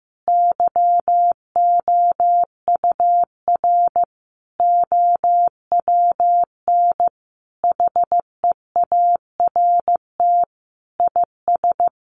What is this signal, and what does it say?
Morse code, heard as a keyed tone, about 15 wpm, 700 Hz. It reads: YOUR OWN HEART IS
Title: Sentences from Top 1200 Words 15wpm